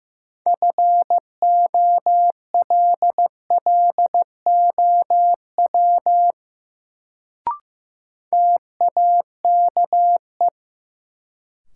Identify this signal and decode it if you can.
Morse code, heard as a keyed tone, about 15 wpm, 700 Hz. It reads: FOLLOW TAKE
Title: Sentences from Top 200 Words - New Word Drill - 15wpm